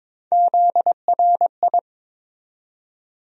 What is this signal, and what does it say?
Morse code, heard as a keyed tone, about 22 wpm, 700 Hz. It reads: ZRI
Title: USA Call Sign Postfixes - 3 Letters 22wpm